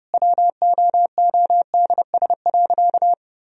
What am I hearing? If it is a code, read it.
Morse code, heard as a keyed tone, about 30 wpm, 700 Hz. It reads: WOODS.
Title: Sentences from Top 1000 Words - Review Part 2 - 30wpm